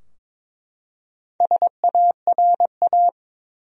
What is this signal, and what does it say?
Morse code, heard as a keyed tone, about 22 wpm, 700 Hz. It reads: SARA
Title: USA Common Names - Max 5 Letters - 22wpm